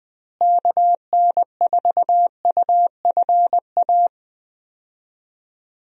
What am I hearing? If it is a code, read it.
Morse code, heard as a keyed tone, about 21 wpm, 700 Hz. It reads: KN4UFA
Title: USA Call Signs - 6 Characters - Repeated 3x - 20wpm